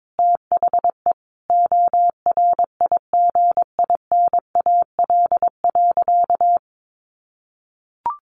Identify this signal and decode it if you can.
Morse code, heard as a keyed tone, about 22 wpm, 700 Hz. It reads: THE ORIGINAL.
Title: Sentences from Top 900 Words 22wpm